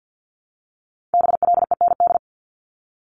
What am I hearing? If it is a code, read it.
Morse code, heard as a keyed tone, about 50 wpm, 700 Hz. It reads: BLEND
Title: Multi-letter Phonemes 50wpm